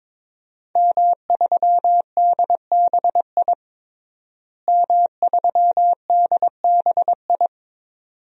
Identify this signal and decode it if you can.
Morse code, heard as a keyed tone, about 22 wpm, 700 Hz. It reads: M3DBI M3DBI
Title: DXCC-100 Call Signs - 5 Characters - Repeated 2x - 22wpm